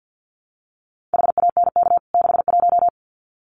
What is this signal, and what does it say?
Morse code, heard as a keyed tone, about 50 wpm, 700 Hz. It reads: HANK 61
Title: CWT Exchange 50wpm